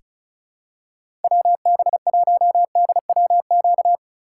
Morse code, heard as a keyed tone, about 36 wpm, 700 Hz. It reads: WB1DWQ
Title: USA Call Signs - 6 Characters - 2x Character Spacing - 35wpm